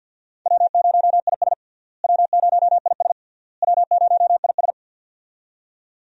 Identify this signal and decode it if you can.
Morse code, heard as a keyed tone, about 51 wpm, 700 Hz. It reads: W0IS W0IS W0IS
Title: USA Call Signs - 4 Characters - Repeated 3x - 50wpm